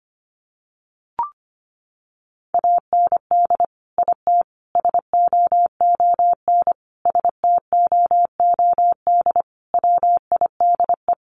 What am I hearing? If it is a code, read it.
Morse code, heard as a keyed tone, about 25 wpm, 700 Hz. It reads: AND IT SOON STOOD WIDE
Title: Sentences from Top 900 Words 25wpm